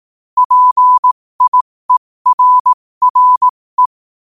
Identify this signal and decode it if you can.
Morse code, heard as a keyed tone, about 18 wpm, spaced as for 15 wpm, 1000 Hz. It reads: PIERRE
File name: qotdfiles / 15wpm